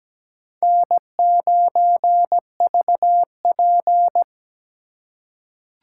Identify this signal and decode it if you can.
Morse code, heard as a keyed tone, about 17 wpm, 700 Hz. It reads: N9VP
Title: USA Call Signs - 3x Character Spacing - 17wpm